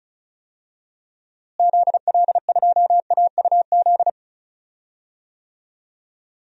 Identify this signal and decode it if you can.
Morse code, heard as a keyed tone, about 36 wpm, 700 Hz. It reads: ZL2AUZ
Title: DXCC-100 Call Signs - 6 Characters - Repeated 3x - 35wpm